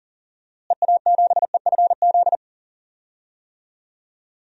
Morse code, heard as a keyed tone, about 40 wpm, 700 Hz. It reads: EA7EFZ
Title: DXCC-100 Call Signs - 6 Characters - Repeated 2x - 40wpm